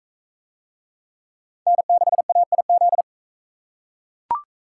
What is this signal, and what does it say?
Morse code, heard as a keyed tone, about 41 wpm, 700 Hz. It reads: N6AIZ